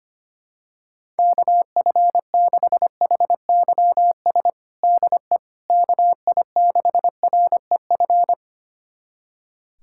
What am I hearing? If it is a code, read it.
Morse code, heard as a keyed tone, about 25 wpm, 700 Hz. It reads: KF6HYS DE KI6REF
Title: USA Callsign DE Callsign - 6 Characters - 25wpm